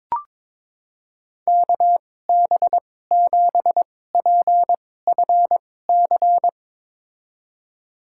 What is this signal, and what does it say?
Morse code, heard as a keyed tone, about 22 wpm, spaced as for 16 wpm, 700 Hz. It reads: KB7PFC
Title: USA Call Signs - 6 Characters - 2x Character Spacing - 22wpm